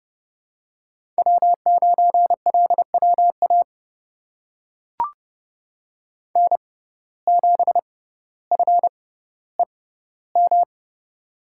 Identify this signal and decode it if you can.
Morse code, heard as a keyed tone, about 30 wpm, spaced as for 15 wpm, 700 Hz. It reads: W9LWA N7FEM
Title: USA Call Signs - 5 Characters - 6x Character Spacing - 30wpm